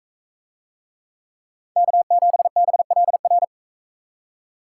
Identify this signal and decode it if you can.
Morse code, heard as a keyed tone, about 41 wpm, 700 Hz. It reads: K7BLR